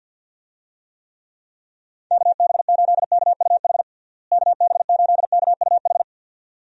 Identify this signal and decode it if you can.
Morse code, heard as a keyed tone, about 51 wpm, 700 Hz. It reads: KB8KUH KB8KUH
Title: USA Call Signs - 6 Characters - Repeated 2x - 50wpm